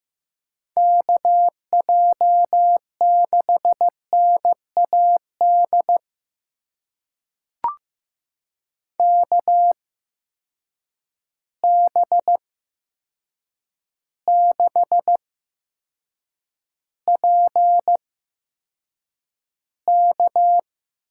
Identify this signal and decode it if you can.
Morse code, heard as a keyed tone, about 15 wpm, spaced as for 7 wpm, 700 Hz. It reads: KJ6NAD KB6PK
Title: USA Call Signs - 6 Characters - 8x Character Spacing - 15wpm